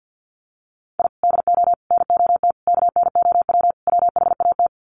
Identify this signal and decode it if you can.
Morse code, heard as a keyed tone, about 50 wpm, 700 Hz. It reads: I DO NOT KNOW WHAT
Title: Sets of 5 Words 50wpm